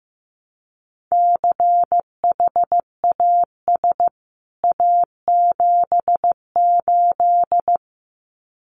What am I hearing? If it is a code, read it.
Morse code, heard as a keyed tone, about 15 wpm, 700 Hz. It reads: CHAS A78
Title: CWT Exchange 15wpm